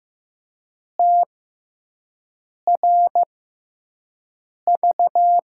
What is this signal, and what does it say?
Morse code, heard as a keyed tone, about 15 wpm, spaced as for 5 wpm, 700 Hz. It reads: TRV
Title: USA Call Signs - 6x Character Spacing - 15wpm